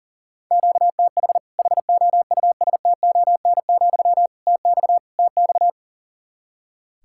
Morse code, heard as a keyed tone, about 40 wpm, 700 Hz. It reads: QTH HOUSTON, TX TX
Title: Practice QSO - Exchange 1 - Beginning - 40wpm